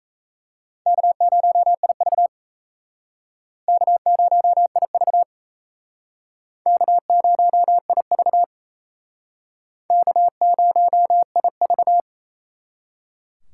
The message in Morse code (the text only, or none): K0IV K0IV K0IV K0IV